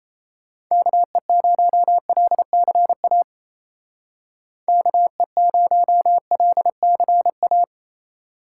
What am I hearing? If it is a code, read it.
Morse code, heard as a keyed tone, about 36 wpm, 700 Hz. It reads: KE0LCA KE0LCA